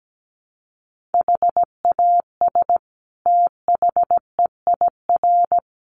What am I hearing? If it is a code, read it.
Morse code, heard as a keyed tone, about 17 wpm, 700 Hz. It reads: HAS THEIR
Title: Sentences from Top 1500 Words 17wpm